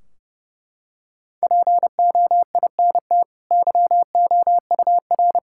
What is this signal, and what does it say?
Morse code, heard as a keyed tone, about 30 wpm, 700 Hz. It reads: POINT YOUR
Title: Sentences from Top 1600 Words 30wpm